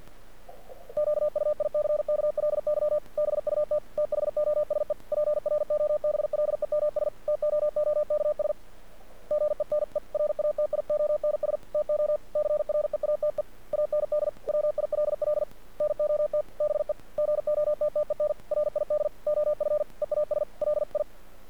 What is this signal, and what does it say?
Morse code, heard as a keyed tone, about 50 wpm, 600 Hz. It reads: QUICKLY BUT THOSE PROBLEMS TOOKS GENERATIONS TO CREATE AND WILL NOT BE GOTTEN RID OF EASLI
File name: archived / CRSnet